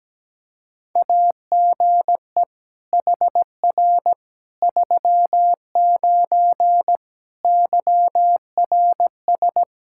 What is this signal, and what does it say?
Morse code, heard as a keyed tone, about 17 wpm, 700 Hz. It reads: AGE HR 39 YRS
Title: Practice QSO - Exchange 3 - More Info - 17wpm